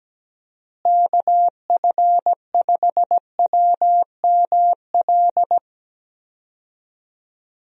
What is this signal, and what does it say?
Morse code, heard as a keyed tone, about 17 wpm, 700 Hz. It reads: KF5WML
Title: USA Call Signs - 6 Characters - Repeated 3x - 17wpm